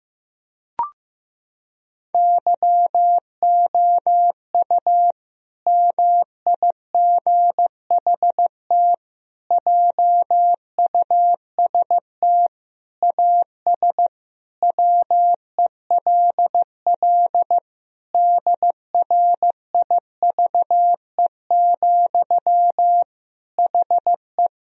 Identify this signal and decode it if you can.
Morse code, heard as a keyed tone, about 15 wpm, 700 Hz. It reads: YOU MIGHT JUST AS WELL DRIVE, HE
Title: Sentences from Top 1000 Words - Review Part 2 - 15wpm